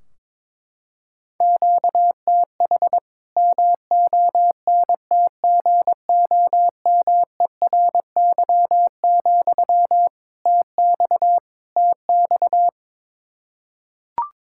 Morse code, heard as a keyed tone, about 22 wpm, 700 Hz. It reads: QTH MONTGOMERY, TX TX
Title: Practice QSO - Exchange 1 - Beginning - 22wpm